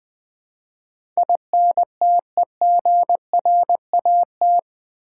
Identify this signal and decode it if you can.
Morse code, heard as a keyed tone, about 20 wpm, 700 Hz. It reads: INTEGRAT
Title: Build A Word - 4 Letters At A Time - 11-Letter Words - 20wpm